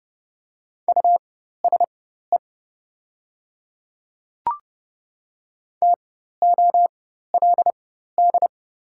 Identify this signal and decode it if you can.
Morse code, heard as a keyed tone, about 30 wpm, spaced as for 14 wpm, 700 Hz. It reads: USE TOLD
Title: 4x Character Spacing - Top 500 Words - 30wpm